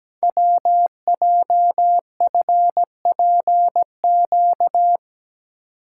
Morse code, heard as a keyed tone, about 17 wpm, 700 Hz. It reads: WJFPQ
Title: Code Groups - 5 Letters - 17wpm